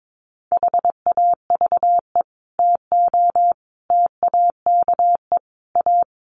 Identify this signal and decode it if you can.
Morse code, heard as a keyed tone, about 22 wpm, 700 Hz. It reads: HAVE TO TAKE A
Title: Sets of 5 Words Encore 22wpm